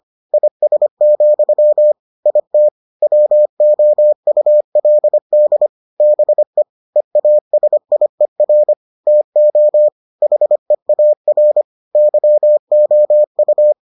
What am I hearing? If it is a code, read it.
Morse code, heard as a keyed tone, about 25 wpm, 600 Hz. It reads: IS, IT WOULD BE EASIER TO HEAR YOU